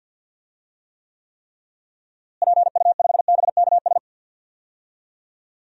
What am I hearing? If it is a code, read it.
Morse code, heard as a keyed tone, about 47 wpm, 700 Hz. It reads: PU5BKS
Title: DXCC-100 Call Signs - 6 Characters - Repeated 3x - 50wpm